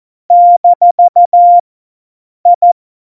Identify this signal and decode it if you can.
Morse code, heard as a keyed tone, about 14 wpm, 700 Hz. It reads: - I
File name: Morsekurs / Witze